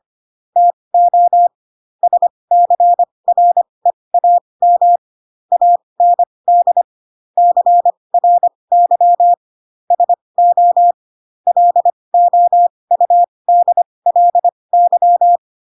Morse code, heard as a keyed tone, about 25 wpm, spaced as for 20 wpm, 700 Hz. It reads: TO SCREAM AND CRY SO LOUDLY
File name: Pinocchio / Pinocchio ch7 15 25 ik1hge